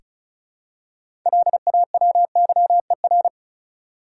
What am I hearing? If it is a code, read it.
Morse code, heard as a keyed tone, about 36 wpm, 700 Hz. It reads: LAWYER
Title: Six Letter Words - 35wpm